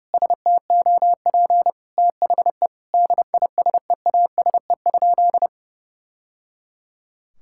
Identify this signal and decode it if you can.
Morse code, heard as a keyed tone, about 30 wpm, 700 Hz. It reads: STOP THE DISEASE?
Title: Sentences from Top 1000 Words - Review Part 6 - 30wpm